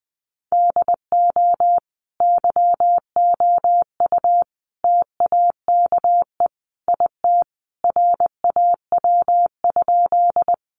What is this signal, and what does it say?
Morse code, heard as a keyed tone, about 20 wpm, 700 Hz. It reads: DO YOU TAKE IT RAW?
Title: Sentences from Top 1500 Words 20wpm